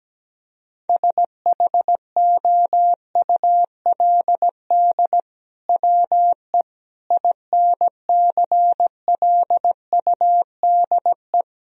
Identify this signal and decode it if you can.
Morse code, heard as a keyed tone, about 17 wpm, 700 Hz. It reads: SHOULD WE INCLUDE
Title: Sentences from Top 1000 Words - Review Part 3 - 17wpm